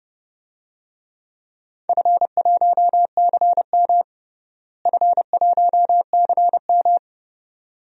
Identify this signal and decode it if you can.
Morse code, heard as a keyed tone, about 29 wpm, 700 Hz. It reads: F1CM F1CM
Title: DXCC-100 Call Signs - 4 Characters - Repeated 2x - 30wpm